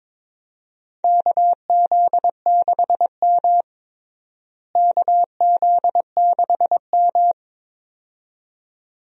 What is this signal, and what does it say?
Morse code, heard as a keyed tone, about 22 wpm, 700 Hz. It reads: KZ6M KZ6M
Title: USA Call Signs - 4 Characters - Repeated 2x - 22wpm